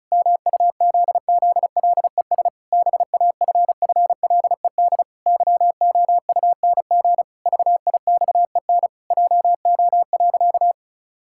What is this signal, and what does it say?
Morse code, heard as a keyed tone, about 34 wpm, 700 Hz. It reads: MUZZLES BAFFLED YOUNG VIXEN JO.
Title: Pangram Challenge v2 - 35wpm